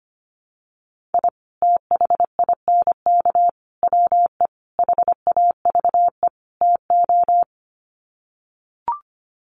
Morse code, heard as a keyed tone, about 25 wpm, 700 Hz. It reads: I THINK WE HAVE TO
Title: Sets of 5 Words 25wpm